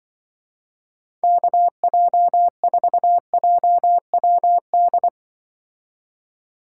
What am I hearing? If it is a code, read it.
Morse code, heard as a keyed tone, about 23 wpm, 700 Hz. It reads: KJ4JWD
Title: Speed Racing - USA 2x3 Call Signs 20wpm